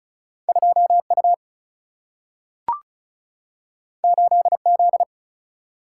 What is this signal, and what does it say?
Morse code, heard as a keyed tone, about 36 wpm, 700 Hz. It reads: 2 U 8 Z
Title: Single-Letter Single-Number 35wpm